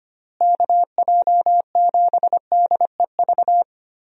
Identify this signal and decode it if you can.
Morse code, heard as a keyed tone, about 25 wpm, 700 Hz. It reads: KJ7DEV
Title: USA Call Signs - 6 Characters - Repeated 2x - 25wpm